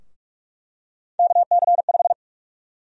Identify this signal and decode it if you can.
Morse code, heard as a keyed tone, about 46 wpm, 700 Hz. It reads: KC5
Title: USA Call Sign Prefixes - 45wpm